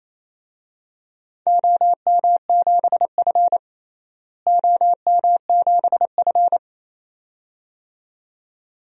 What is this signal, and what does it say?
Morse code, heard as a keyed tone, about 28 wpm, 700 Hz. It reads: OM7F OM7F
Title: DXCC-100 Call Signs - 4 Characters - Repeated 2x - 28wpm